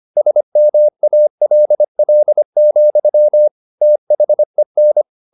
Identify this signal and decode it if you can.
Morse code, heard as a keyed tone, about 25 wpm, 600 Hz. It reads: SMALL, THEN